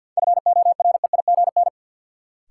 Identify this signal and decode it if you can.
Morse code, heard as a keyed tone, about 52 wpm, 700 Hz. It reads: FOREIGN
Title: Sentences from Top 1400 Words - New Word Drill Alt - 50wpm